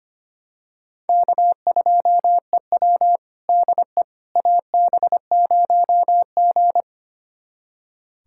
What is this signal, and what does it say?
Morse code, heard as a keyed tone, about 25 wpm, 700 Hz. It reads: K2EW DE AB0G
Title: USA Callsign DE Callsign - 4 Characters - 25wpm